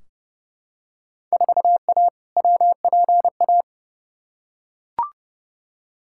Morse code, heard as a keyed tone, about 30 wpm, 700 Hz. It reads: 4A WPA
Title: Field Day Exchanges Encore - 30wpm